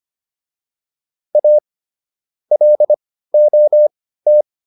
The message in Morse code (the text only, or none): A LOT